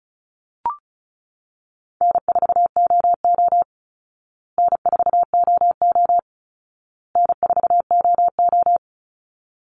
Morse code, heard as a keyed tone, about 34 wpm, 700 Hz. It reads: N4OO N4OO N4OO
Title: Easy Call Signs Repeated 3x - 35wpm